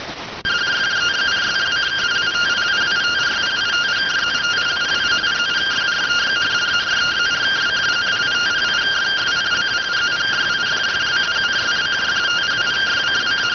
Two-tone teleprinter signal, about 170 Hz shift.